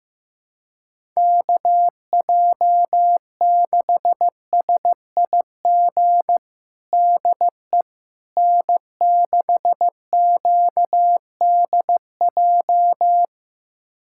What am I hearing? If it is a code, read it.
Morse code, heard as a keyed tone, about 15 wpm, 700 Hz. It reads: KJ6SIG DE N6QDJ
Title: USA Callsign DE Callsign 15wpm